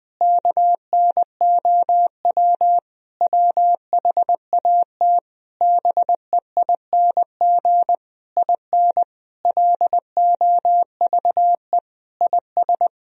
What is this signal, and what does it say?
Morse code, heard as a keyed tone, about 20 wpm, 700 Hz. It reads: KNOW WHAT BEING IN LOVE IS
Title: Sentences from Top 1000 Words - Review Part 3 - 20wpm